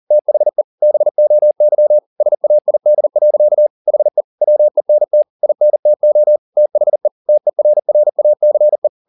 Morse code, heard as a keyed tone, about 40 wpm, 600 Hz. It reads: THE BOY SAID. HE WENT INTO THE TERRACE
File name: the old man and the sea 2 40wpm